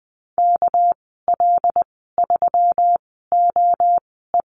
Morse code, heard as a keyed tone, about 20 wpm, spaced as for 15 wpm, 700 Hz. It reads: KL3OE
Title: 200 Lesson 32g - Letter k - Call Signs 20wpm